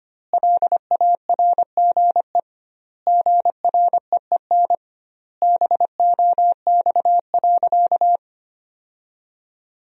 Morse code, heard as a keyed tone, about 25 wpm, 700 Hz. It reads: LARGE GREEN BOX.
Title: 2x Word-Spacing - Sentences from Top 500 Words - 25wpm